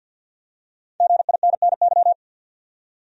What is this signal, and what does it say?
Morse code, heard as a keyed tone, about 51 wpm, 700 Hz. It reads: GINNY
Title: USA Common Names - 50wpm